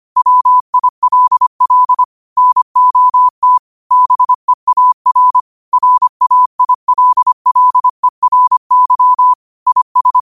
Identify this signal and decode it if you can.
Morse code, heard as a keyed tone, about 25 wpm, 1000 Hz. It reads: WILL NOT BEAR RAILLERY IS
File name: qotdfiles / 25wpm